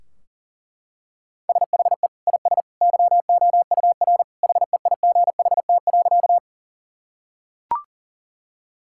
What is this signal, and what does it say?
Morse code, heard as a keyed tone, about 40 wpm, 700 Hz. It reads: SHE IS YOUR HEIGHT.
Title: Sentences from Top 1600 Words 40wpm